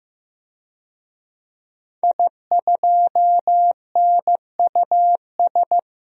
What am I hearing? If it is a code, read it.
Morse code, heard as a keyed tone, about 15 wpm, 700 Hz. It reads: I2NUS
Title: DXCC-100 Call Signs - 5 Characters - Repeated 2x - 15wpm